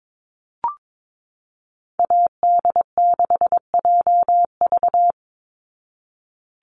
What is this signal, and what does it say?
Morse code, heard as a keyed tone, about 22 wpm, 700 Hz. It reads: AD6JV
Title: Call Signs Repeated 3x - 22wpm